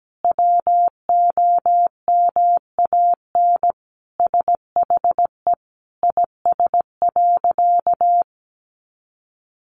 Morse code, heard as a keyed tone, about 17 wpm, 700 Hz. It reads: WOMAN SHE IS.
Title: Sentences from Top 1400 Words 17wpm